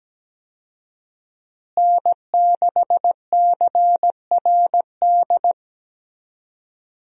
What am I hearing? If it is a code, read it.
Morse code, heard as a keyed tone, about 17 wpm, 700 Hz. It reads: N6CRD
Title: USA Call Signs - 5 Characters - Repeated 3x - 17wpm